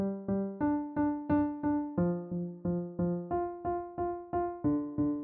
There is no code.